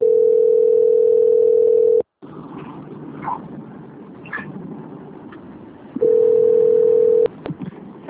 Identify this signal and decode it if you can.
Ringback tone (North America), 440 + 480 Hz.